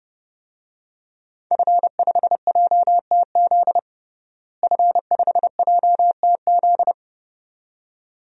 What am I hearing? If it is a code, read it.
Morse code, heard as a keyed tone, about 30 wpm, 700 Hz. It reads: F5JTZ F5JTZ
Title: DXCC-100 Call Signs - 5 Characters - Repeated 2x - 30wpm